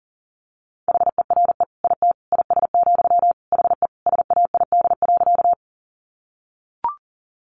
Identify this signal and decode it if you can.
Morse code, heard as a keyed tone, about 40 wpm, 700 Hz. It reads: HERE IT IS, HE SAID.
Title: Sentences from Top 200 Words 40wpm